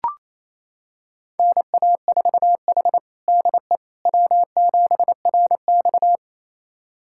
Morse code, heard as a keyed tone, about 27 wpm, 700 Hz. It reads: NA4H DE W7RX
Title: USA Callsign DE Callsign - 4 Characters - 28wpm